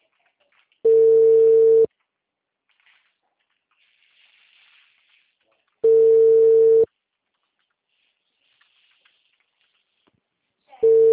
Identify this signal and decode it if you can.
Ringback tone (China, 450 Hz).
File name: out- -2013- - - .22416